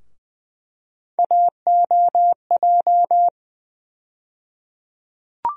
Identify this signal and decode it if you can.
Morse code, heard as a keyed tone, about 20 wpm, 700 Hz. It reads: AOJ
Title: USA Call Sign Postfixes - 3 Letters 20wpm